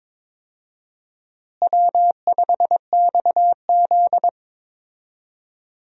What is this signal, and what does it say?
Morse code, heard as a keyed tone, about 22 wpm, 700 Hz. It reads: W5XZ
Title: USA Call Signs - 4 Characters - Repeated 2x - 22wpm